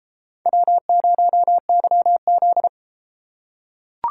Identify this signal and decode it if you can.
Morse code, heard as a keyed tone, about 30 wpm, 700 Hz. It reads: W0YZ